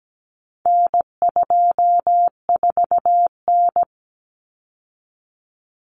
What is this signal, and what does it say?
Morse code, heard as a keyed tone, about 17 wpm, 700 Hz. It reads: N24N
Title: Common Cut Numbers - 4-Digit 17wpm